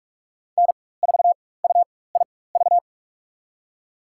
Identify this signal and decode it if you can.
Morse code, heard as a keyed tone, about 46 wpm, spaced as for 21 wpm, 700 Hz. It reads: N4UIV
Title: USA Call Signs - 5 Characters - 4x Character Spacing - 45wpm